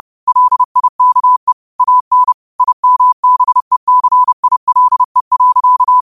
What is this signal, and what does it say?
Morse code, heard as a keyed tone, about 30 wpm, 1000 Hz. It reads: LIKE AN IMBECILE.
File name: qotdfiles / 30wpm